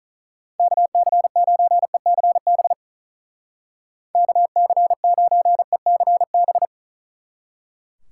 Morse code, heard as a keyed tone, about 44 wpm, 700 Hz. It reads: KC9ECB KC9ECB
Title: Speed Racing - USA 2x3 Call Signs 35wpm